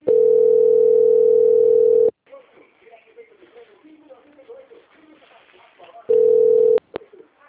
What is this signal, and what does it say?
Ringback tone (North America), 440 + 480 Hz, with speech.
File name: out- - - - - .25845